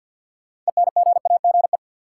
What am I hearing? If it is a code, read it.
Morse code, heard as a keyed tone, about 50 wpm, 700 Hz. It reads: ENGAGE
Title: Sentences from Top 1500 Words - New Word Drill Alt - 50wpm